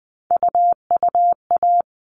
Morse code, heard as a keyed tone, about 20 wpm, 700 Hz. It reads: UUA
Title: All Cut Numbers - 3-Digit 20wpm